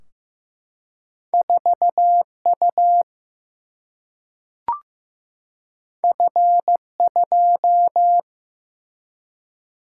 Morse code, heard as a keyed tone, about 15 wpm, 700 Hz. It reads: 4U F2
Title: Single-Letter Single-Number 15wpm